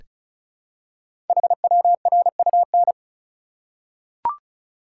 Morse code, heard as a keyed tone, about 35 wpm, 700 Hz. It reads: HWRUN
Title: Code Groups - 5 Letters - 35wpm